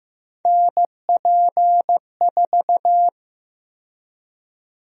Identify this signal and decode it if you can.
Morse code, heard as a keyed tone, about 15 wpm, 700 Hz. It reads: NP4
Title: USA Call Sign Prefixes - 15wpm